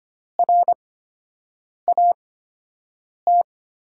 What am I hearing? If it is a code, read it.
Morse code, heard as a keyed tone, about 25 wpm, spaced as for 7 wpm, 700 Hz. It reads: RAT
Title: USA Call Signs - 6 Characters - 8x Character Spacing - 25wpm